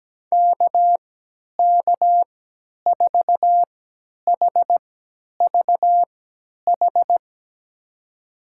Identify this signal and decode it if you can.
Morse code, heard as a keyed tone, about 17 wpm, spaced as for 10 wpm, 700 Hz. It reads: KK4HVH
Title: USA Call Signs - 6 Characters - 3x Character Spacing - 17wpm